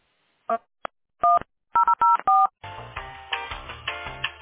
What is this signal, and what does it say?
DTMF, heard as telephone touch tones, tones of about 190 ms. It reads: 1 004